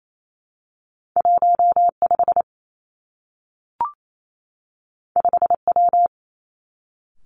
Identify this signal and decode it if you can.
Morse code, heard as a keyed tone, about 28 wpm, 700 Hz. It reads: 1 5 5 W
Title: Two Challenging Letter-Number Combinations 28wpm